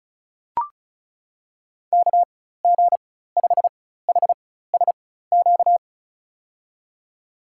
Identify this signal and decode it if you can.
Morse code, heard as a keyed tone, about 36 wpm, spaced as for 16 wpm, 700 Hz. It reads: KG5HSQ
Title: USA Call Signs - 4x Character Spacing - 35wpm